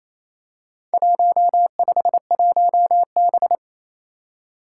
Morse code, heard as a keyed tone, about 28 wpm, 700 Hz. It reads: 151B
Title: Four Challenging Letter-Number Combinations 28wpm